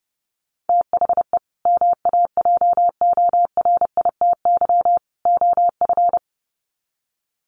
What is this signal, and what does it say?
Morse code, heard as a keyed tone, about 30 wpm, 700 Hz. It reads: THE MAJORITY OF
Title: Sets of 3 Words 30wpm